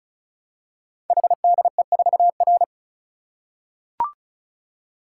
Morse code, heard as a keyed tone, about 36 wpm, 690 Hz. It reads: HDE4R
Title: Code Groups - 5 Letter:Numbers - 35wpm